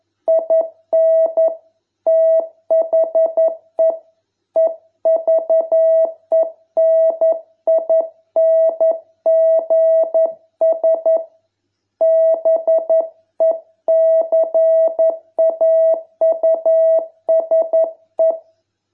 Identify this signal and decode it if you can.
Morse code, heard as a keyed tone, about 11 wpm, 660 Hz. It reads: IN THE EVENINGS BECAUSE